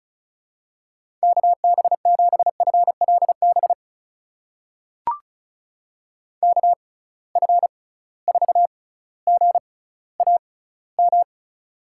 Morse code, heard as a keyed tone, about 34 wpm, spaced as for 18 wpm, 700 Hz. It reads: KB7FLB KF4GAM
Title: USA Call Signs - 6 Characters - 6x Character Spacing - 35wpm